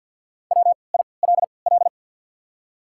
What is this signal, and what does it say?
Morse code, heard as a keyed tone, about 51 wpm, spaced as for 28 wpm, 700 Hz. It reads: WILL
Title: 3x Character Spacing - Top 100 Words - 50wpm